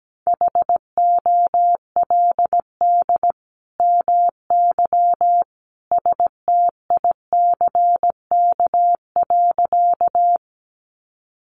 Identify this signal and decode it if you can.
Morse code, heard as a keyed tone, about 17 wpm, 700 Hz. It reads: HOLD MY STICK.
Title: Sentences from Top 1200 Words 17wpm